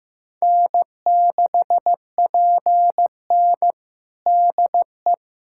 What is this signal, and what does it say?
Morse code, heard as a keyed tone, about 15 wpm, 700 Hz. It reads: N6PN DE
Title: Practice QSO - Exchange 4 - Ending - 15wpm